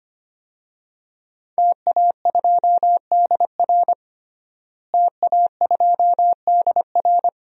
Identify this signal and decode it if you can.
Morse code, heard as a keyed tone, about 25 wpm, 700 Hz. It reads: TA2DR TA2DR
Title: DXCC-100 Call Signs - 5 Characters - Repeated 2x - 25wpm